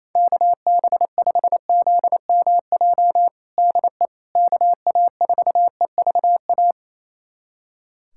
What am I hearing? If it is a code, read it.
Morse code, heard as a keyed tone, about 28 wpm, 700 Hz. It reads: KB5ZMJ DE KA4EVA
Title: USA Callsign DE Callsign - 6 Characters - 28wpm